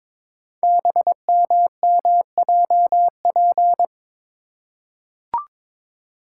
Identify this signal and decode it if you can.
Morse code, heard as a keyed tone, about 23 wpm, 700 Hz. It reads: BMMJP
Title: Code Groups - 5 Letter:Numbers - 22wpm